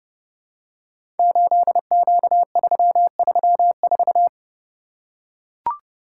Morse code, heard as a keyed tone, about 30 wpm, 700 Hz. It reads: 8Q334
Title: Code Groups - 5 Letter:Numbers - 30wpm